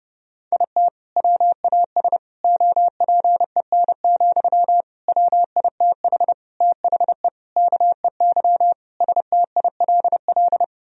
Morse code, heard as a keyed tone, about 30 wpm, 700 Hz. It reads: IT WAS OPEN, WITH THE KEY STILL
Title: Sentences from Top 1000 Words - Review Part 4 - 30wpm